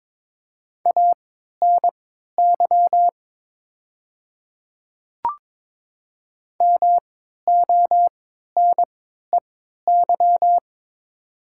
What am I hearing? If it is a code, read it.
Morse code, heard as a keyed tone, about 22 wpm, spaced as for 12 wpm, 700 Hz. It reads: ANY MONEY
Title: 3x Character Spacing - Top 500 Words - 22wpm